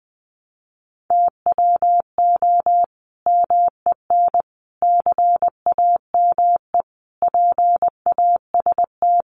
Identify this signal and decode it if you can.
Morse code, heard as a keyed tone, about 20 wpm, 700 Hz. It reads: TWO MEN CAME PAST
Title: Sentences from Top 600 Words 20wpm